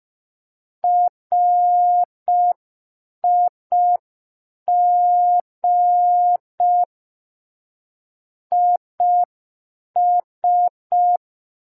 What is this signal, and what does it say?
Morse code, heard as a keyed tone, about 5 wpm, 710 Hz. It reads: RIG IS
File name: g3zpd / Morse code P4